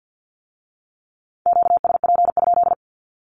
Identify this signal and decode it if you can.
Morse code, heard as a keyed tone, about 50 wpm, 700 Hz. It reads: QSP?
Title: Q-Codes with Spoken Meaning - 50wpm